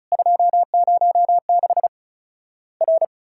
Morse code, heard as a keyed tone, about 34 wpm, 700 Hz. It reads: 206 R
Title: Worked All Europe DX Contest - QTC Exchange - 35wpm